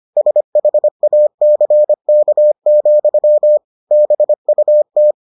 Morse code, heard as a keyed tone, about 25 wpm, 600 Hz. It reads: SHACK, BUT